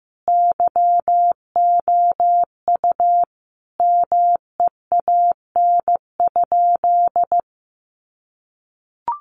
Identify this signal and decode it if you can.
Morse code, heard as a keyed tone, about 14 wpm, 700 Hz. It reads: YOU MEAN?
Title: Sentences from Top 200 Words 15wpm